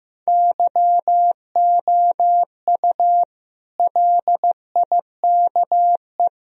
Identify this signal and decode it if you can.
Morse code, heard as a keyed tone, about 15 wpm, 700 Hz. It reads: YOU LIKE
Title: Sentences from Top 1000 Words - Review Part 1 - 15wpm